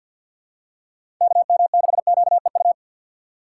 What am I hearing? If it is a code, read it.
Morse code, heard as a keyed tone, about 53 wpm, 700 Hz. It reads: KM6QEU
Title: USA Call Signs - 6 Characters - 3x Character Spacing - 50wpm